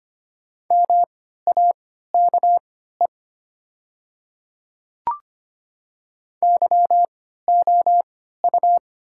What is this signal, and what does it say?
Morse code, heard as a keyed tone, about 25 wpm, spaced as for 14 wpm, 700 Hz. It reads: MAKE YOU
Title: 3x Character Spacing - Top 100 Words - 25wpm